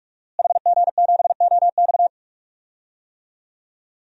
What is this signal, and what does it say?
Morse code, heard as a keyed tone, about 46 wpm, 700 Hz. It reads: HG7OX
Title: DXCC-100 Call Signs - 5 Characters - Repeated 2x - 45wpm